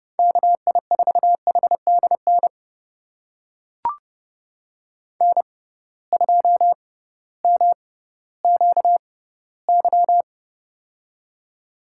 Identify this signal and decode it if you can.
Morse code, heard as a keyed tone, about 30 wpm, spaced as for 16 wpm, 700 Hz. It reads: KI4HDN N2MQY
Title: USA Call Signs - 6x Character Spacing - 30wpm